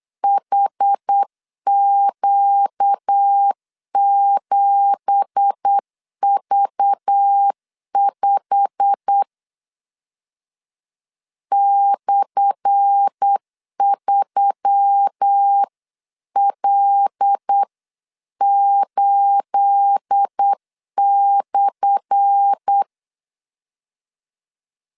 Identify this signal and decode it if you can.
Morse code, heard as a keyed tone, about 8 wpm, 800 Hz. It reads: HQ7V5 /3L8/
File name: LEZIONI / 10 SIGNS / 10A SLASH 8wpm 3452